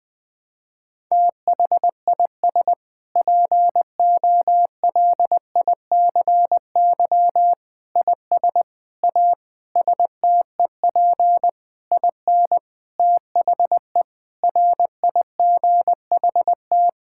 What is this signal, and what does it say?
Morse code, heard as a keyed tone, about 21 wpm, 700 Hz. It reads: THIS POLICY IS A STEP IN THE RIGHT
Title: Sentences with 6-Word Phrases - 20wpm